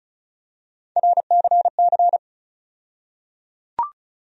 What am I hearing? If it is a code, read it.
Morse code, heard as a keyed tone, about 36 wpm, 700 Hz. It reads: RCC
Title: USA Call Sign Postfixes - 3 Letters 35wpm